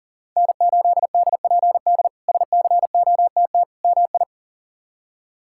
Morse code, heard as a keyed tone, about 40 wpm, 700 Hz. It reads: N8DPD SCOTT MI
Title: K1USN SST QSO Practice - 40wpm